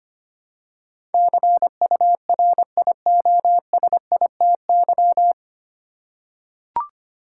Morse code, heard as a keyed tone, about 25 wpm, 700 Hz. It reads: CURIOSITY
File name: Nine Letter Words - 25wpm